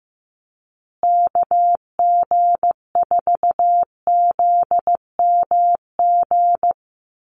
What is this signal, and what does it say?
Morse code, heard as a keyed tone, about 15 wpm, 700 Hz. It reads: KG4ZMG
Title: USA Call Signs Repeated 2x 15wpm